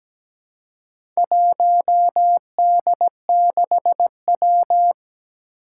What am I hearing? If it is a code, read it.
Morse code, heard as a keyed tone, about 17 wpm, 700 Hz. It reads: 1D6W
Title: Four Challenging Letter-Number Combinations 17wpm